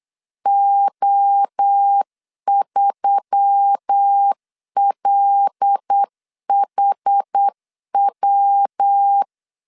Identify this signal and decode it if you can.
Morse code, heard as a keyed tone, about 9 wpm, 800 Hz. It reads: O3LHW
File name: CW / DET / 1 6 1500C S C 8wpm 5219